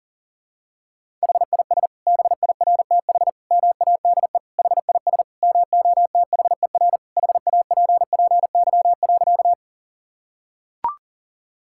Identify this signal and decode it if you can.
Morse code, heard as a keyed tone, about 40 wpm, 700 Hz. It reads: HIS BIRTH MADE HIS MOTHER HAPPY.
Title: Sentences from Top 1000 Words - Review Part 5 - 40wpm